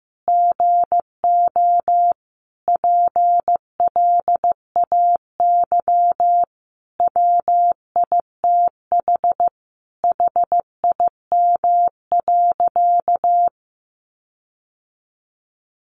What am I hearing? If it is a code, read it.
Morse code, heard as a keyed tone, about 14 wpm, 700 Hz. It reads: GO PLAY WITH HIM.
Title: Sentences from Top 1500 Words 15wpm